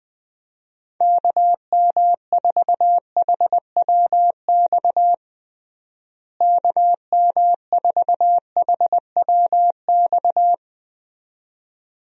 Morse code, heard as a keyed tone, about 20 wpm, 700 Hz. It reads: KM4HWX KM4HWX
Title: USA Call Signs - 6 Characters - Repeated 2x - 20wpm